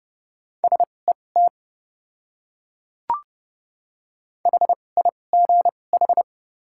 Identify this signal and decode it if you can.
Morse code, heard as a keyed tone, about 29 wpm, spaced as for 22 wpm, 700 Hz. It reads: SET HIGH
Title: 2x Character Spacing - Top 200 Words - 30wpm